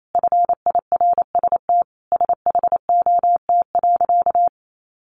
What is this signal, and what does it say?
Morse code, heard as a keyed tone, about 28 wpm, 700 Hz. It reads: FIRST SHOT.
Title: A Sound Of Thunder 28wpm